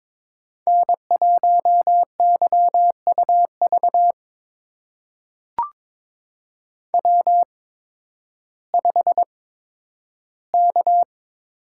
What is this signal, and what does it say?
Morse code, heard as a keyed tone, about 22 wpm, spaced as for 12 wpm, 700 Hz. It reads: N1YUV W5K
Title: USA Call Signs - 5 Characters - 8x Character Spacing - 22wpm